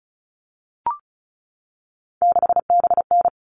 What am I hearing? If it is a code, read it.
Morse code, heard as a keyed tone, about 34 wpm, 700 Hz. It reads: 6BN
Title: All Cut Numbers - 3-Digit 35wpm